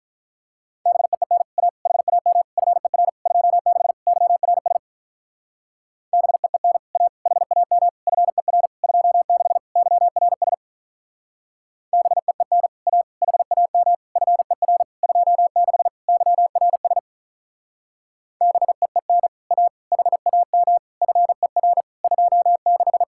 Morse code, heard as a keyed tone, about 46 wpm, 700 Hz. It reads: BEEN A HAM FER 26 YRS BEEN A HAM FER 26 YRS BEEN A HAM FER 26 YRS BEEN A HAM FER 26